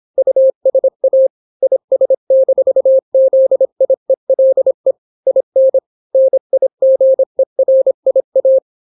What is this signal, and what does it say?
Morse code, heard as a keyed tone, about 26 wpm, 530 Hz. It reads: USA IS-ZIELE IN NIGERIA